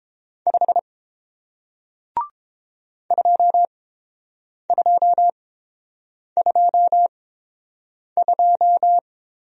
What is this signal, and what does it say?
Morse code, heard as a keyed tone, about 30 wpm, spaced as for 7 wpm, 700 Hz. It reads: 5 2222